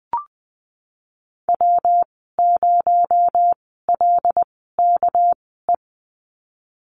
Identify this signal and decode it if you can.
Morse code, heard as a keyed tone, about 20 wpm, spaced as for 15 wpm, 700 Hz. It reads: W0LKE
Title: 231 Lesson 37g - Letter x - Call Signs 20wpm